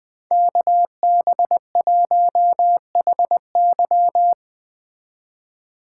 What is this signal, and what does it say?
Morse code, heard as a keyed tone, about 20 wpm, 700 Hz. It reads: KB1HY
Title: Speed Racing - USA 2x2 Call Signs 17wpm